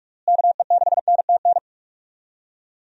Morse code, heard as a keyed tone, about 46 wpm, 700 Hz. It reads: KE6NTN
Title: USA Call Signs - 6 Characters - Repeated 3x - 45wpm